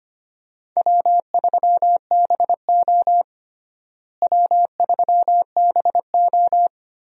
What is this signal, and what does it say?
Morse code, heard as a keyed tone, about 25 wpm, 700 Hz. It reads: W3BO W3BO
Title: USA Call Signs - 4 Characters - Repeated 2x - 25wpm